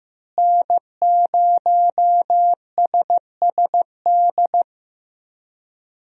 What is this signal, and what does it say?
Morse code, heard as a keyed tone, about 15 wpm, 700 Hz. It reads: N0SSD
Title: USA Call Signs - 5 Characters - Repeated 3x - 15wpm